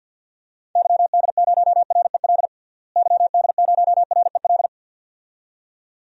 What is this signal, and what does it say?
Morse code, heard as a keyed tone, about 53 wpm, 700 Hz. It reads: YD0REL YD0REL
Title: DXCC-100 Call Signs - 6 Characters - Repeated 2x - 50wpm